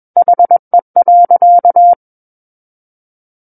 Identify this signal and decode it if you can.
Morse code, heard as a keyed tone, about 21 wpm, 700 Hz. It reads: HE.
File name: Book One / 12-Chapter 12 - 21 WPM